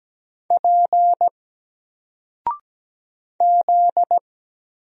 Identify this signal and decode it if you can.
Morse code, heard as a keyed tone, about 17 wpm, 700 Hz. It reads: P Z
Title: Single Character (Letter:Number:Punctuation:Prosign) 17wpm